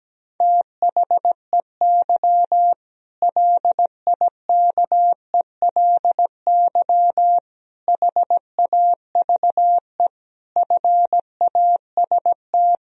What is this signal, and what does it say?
Morse code, heard as a keyed tone, about 17 wpm, 700 Hz. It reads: THEY LIKELY HAVE FAST
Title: Sentences from Top 1000 Words - Review Part 4 - 17wpm